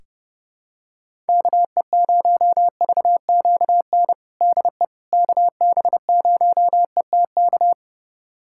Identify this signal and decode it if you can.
Morse code, heard as a keyed tone, about 30 wpm, 700 Hz. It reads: KE0VQN DE KB0ETK
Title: USA Callsign DE Callsign 30wpm